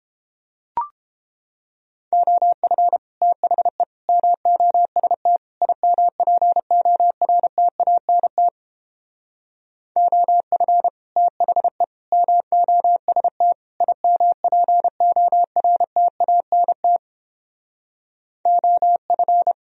Morse code, heard as a keyed tone, about 29 wpm, 700 Hz. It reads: OF THE MOST IMPORTANT OF THE MOST IMPORTANT OF